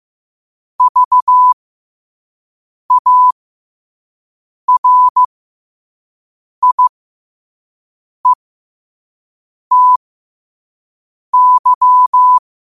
Morse code, heard as a keyed tone, about 15 wpm, spaced as for 5 wpm, 1000 Hz. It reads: VARIETY